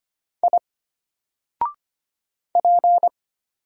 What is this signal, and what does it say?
Morse code, heard as a keyed tone, about 25 wpm, 700 Hz. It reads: I P
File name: Single-Letter-Number-25wpm